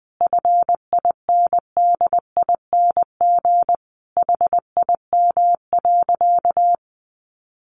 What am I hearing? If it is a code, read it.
Morse code, heard as a keyed tone, about 20 wpm, 700 Hz. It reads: FINDING HIM.
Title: Sentences from Top 1100 Words 20wpm